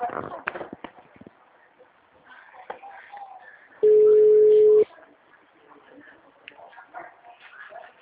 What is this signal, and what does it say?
Ringback tone (Europe, 425 Hz).